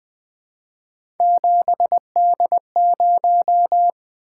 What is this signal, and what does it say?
Morse code, heard as a keyed tone, about 20 wpm, 700 Hz. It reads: 7D0
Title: Three Letter-Number Combinations - 20wpm